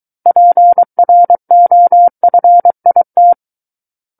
Morse code, heard as a keyed tone, about 23 wpm, 700 Hz. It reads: PROFIT
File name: Book One / 14-Chapter 14 - 23 WPM